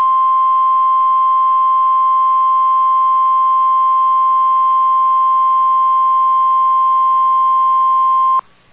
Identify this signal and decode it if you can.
Emergency Alert System (SAME) tones (the 1050 Hz attention tone). It is heard from 1.2 to 8.3 s.